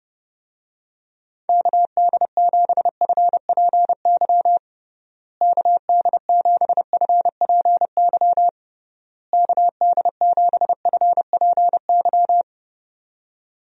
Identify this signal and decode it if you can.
Morse code, heard as a keyed tone, about 30 wpm, 700 Hz. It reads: KD7FPY KD7FPY KD7FPY
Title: USA Call Signs - 6 Characters - Repeated 3x - 30wpm